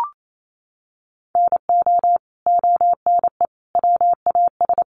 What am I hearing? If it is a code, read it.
Morse code, heard as a keyed tone, about 28 wpm, 700 Hz. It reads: NO ONE WAS
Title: Sentences from Top 1200 Words 28wpm